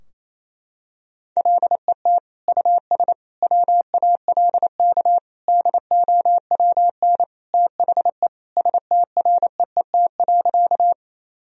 Morse code, heard as a keyed tone, about 27 wpm, 700 Hz. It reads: LET US WALK DOWN THE STREET.
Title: Sentences from Top 1000 Words - Review Part 3 - 28wpm